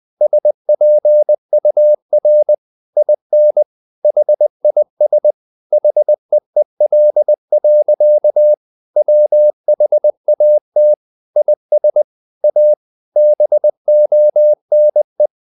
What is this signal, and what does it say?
Morse code, heard as a keyed tone, about 20 wpm, 600 Hz. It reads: SPUR IN HIS HEEL. WHAT IS A BONE
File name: the old man and the sea / the old man and the sea part 1 20wpm